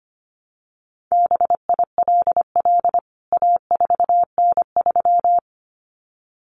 Morse code, heard as a keyed tone, about 25 wpm, 700 Hz. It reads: BILL A4N3
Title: CWT Exchange 25wpm